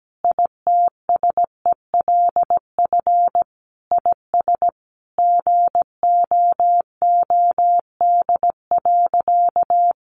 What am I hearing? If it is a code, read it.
Morse code, heard as a keyed tone, about 17 wpm, 700 Hz. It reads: ITSELF IS GOOD.
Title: Sentences from Top 800 Words 17wpm